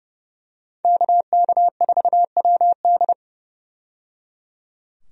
Morse code, heard as a keyed tone, about 29 wpm, 700 Hz. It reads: KK4WD
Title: USA Call Signs - 5 Characters - 30wpm